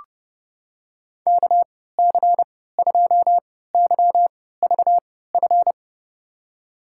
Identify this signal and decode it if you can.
Morse code, heard as a keyed tone, about 32 wpm, spaced as for 17 wpm, 700 Hz. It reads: KC2YVF
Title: USA Call Signs - 3x Character Spacing - 30wpm